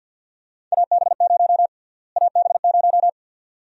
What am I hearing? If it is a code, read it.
Morse code, heard as a keyed tone, about 52 wpm, 700 Hz. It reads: AB0 AB0
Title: USA Call Sign Prefixes - Repeated 2x - 50wpm